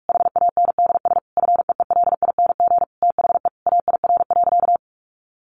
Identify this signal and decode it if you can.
Morse code, heard as a keyed tone, about 44 wpm, 700 Hz. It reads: HANDS FEELING THE AIR.
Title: A Sound Of Thunder 45wpm